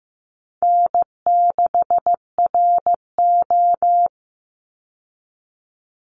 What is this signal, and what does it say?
Morse code, heard as a keyed tone, about 15 wpm, 700 Hz. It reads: N6RO
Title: Call Signs Repeated 3x - 15wpm